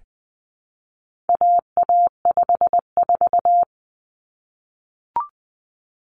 Morse code, heard as a keyed tone, about 20 wpm, 700 Hz. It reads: AA54
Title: Common Cut Numbers - 4-Digit 20wpm